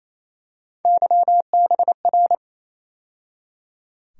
Morse code, heard as a keyed tone, about 28 wpm, 700 Hz. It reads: YBR
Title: USA Call Sign Postfixes - 3 Letters 28wpm